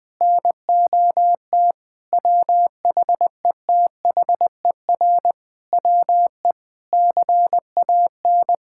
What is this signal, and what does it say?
Morse code, heard as a keyed tone, about 21 wpm, 700 Hz. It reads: NOT WHETHER WE CAN
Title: Sentences from Top 1000 Words - Review Part 6 - 20wpm